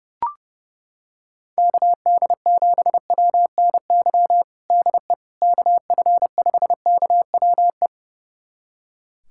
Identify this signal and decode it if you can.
Morse code, heard as a keyed tone, about 32 wpm, 700 Hz. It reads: KD7WNY DE KF5KWE
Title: USA Callsign DE Callsign - 6 Characters - 30wpm